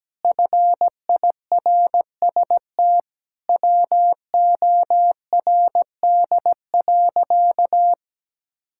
Morse code, heard as a keyed tone, about 17 wpm, 700 Hz. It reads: FIRST WORD.
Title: Pangram Challenge v2 - 17wpm